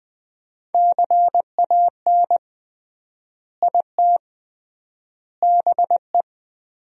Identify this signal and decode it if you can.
Morse code, heard as a keyed tone, about 20 wpm, 700 Hz. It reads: CAN IT BE
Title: 3x Word-Spacing - Sentences from Top 500 Words - 20wpm